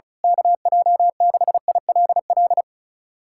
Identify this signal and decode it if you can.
Morse code, heard as a keyed tone, about 34 wpm, 700 Hz. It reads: KJ6ILL
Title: USA Callsign DE Callsign 35wpm